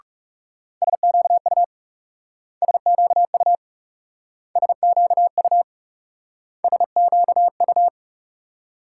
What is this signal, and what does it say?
Morse code, heard as a keyed tone, about 38 wpm, 700 Hz. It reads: SQU SQU SQU SQU